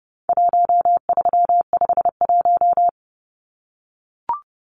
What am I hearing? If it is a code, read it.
Morse code, heard as a keyed tone, about 30 wpm, 700 Hz. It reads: 1351
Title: Numbers - 4-Digit 30wpm